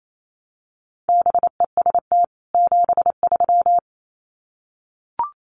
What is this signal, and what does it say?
Morse code, heard as a keyed tone, about 28 wpm, 700 Hz. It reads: BEST 73
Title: Common QSO Phrases 28wpm